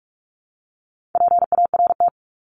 Morse code, heard as a keyed tone, about 44 wpm, 700 Hz. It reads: PART
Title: ES:S Suffix - Top 100 Words with Base Words 45wpm